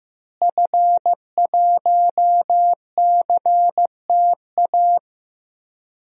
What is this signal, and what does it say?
Morse code, heard as a keyed tone, about 15 wpm, 700 Hz. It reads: F1CTA
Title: DXCC-100 Call Signs - 5 Characters - Repeated 2x - 15wpm